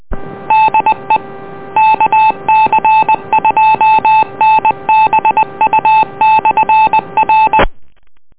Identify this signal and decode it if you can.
Morse code, heard as a keyed tone, about 20 wpm, 870 Hz. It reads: DE KC2NBU/R